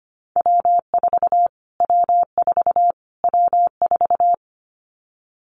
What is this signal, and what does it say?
Morse code, heard as a keyed tone, about 25 wpm, 700 Hz. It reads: W4 W4 W4
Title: Callsign Prefixes for Top 25 DXCC - Repeated 3x - 25wpm